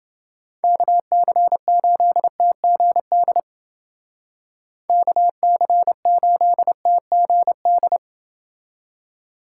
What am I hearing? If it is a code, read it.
Morse code, heard as a keyed tone, about 28 wpm, 700 Hz. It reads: KC8TGD KC8TGD